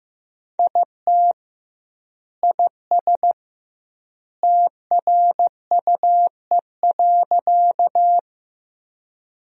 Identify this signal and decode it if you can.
Morse code, heard as a keyed tone, about 15 wpm, 700 Hz. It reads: IT IS TRUE.
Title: 2x Word-Spacing - Sentences from Top 400 Words - 15wpm